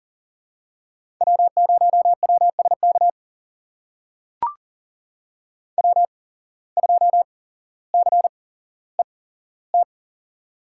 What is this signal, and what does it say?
Morse code, heard as a keyed tone, about 42 wpm, spaced as for 17 wpm, 700 Hz. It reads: W0WSK W2CET
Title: USA Call Signs - 5 Characters - 8x Character Spacing - 40wpm